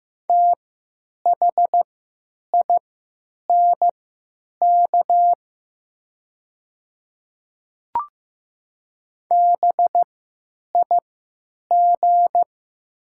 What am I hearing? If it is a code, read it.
Morse code, heard as a keyed tone, about 15 wpm, spaced as for 8 wpm, 700 Hz. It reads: THINK BIG
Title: 3x Character Spacing - Top 500 Words - 15wpm